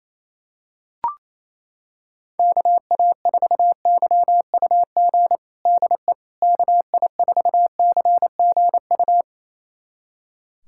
Morse code, heard as a keyed tone, about 28 wpm, 700 Hz. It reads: KA4YUG DE KI4CGU
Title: USA Callsign DE Callsign - 6 Characters - 28wpm